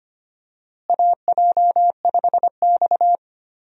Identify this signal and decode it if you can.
Morse code, heard as a keyed tone, about 25 wpm, 700 Hz. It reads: AJ5X
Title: USA Call Signs - 4 Characters - Repeated 2x - 25wpm